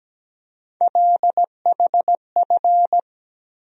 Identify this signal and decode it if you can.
Morse code, heard as a keyed tone, about 17 wpm, 700 Hz. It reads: LHF
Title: USA Call Sign Postfixes - 3 Letters 17wpm